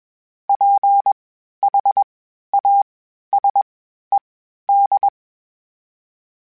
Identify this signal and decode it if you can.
Morse code, heard as a keyed tone, about 21 wpm, spaced as for 12 wpm, 800 Hz. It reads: PHASED
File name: (spaced) / FromSMs spaced mp3s / Antennas sp@25